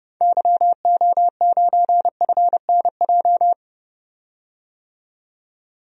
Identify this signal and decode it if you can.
Morse code, heard as a keyed tone, about 30 wpm, 700 Hz. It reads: YO9FNJ
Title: DXCC-100 Call Signs - 6 Characters - 30wpm